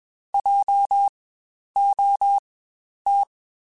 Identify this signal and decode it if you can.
Morse code, heard as a keyed tone, about 21 wpm, spaced as for 10 wpm, 800 Hz. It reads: JOT
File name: Spaced Alphas / Spaced Alphas mp3s / Js sp @20